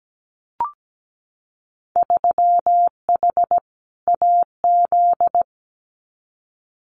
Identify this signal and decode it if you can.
Morse code, heard as a keyed tone, about 17 wpm, 700 Hz. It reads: 3H AZ
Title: Winter Field Day Exchanges Encore 17wpm